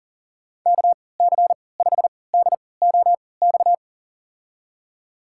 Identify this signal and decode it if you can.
Morse code, heard as a keyed tone, about 40 wpm, spaced as for 23 wpm, 700 Hz. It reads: KC5DOX
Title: USA Call Signs - 6 Characters - 3x Character Spacing - 40wpm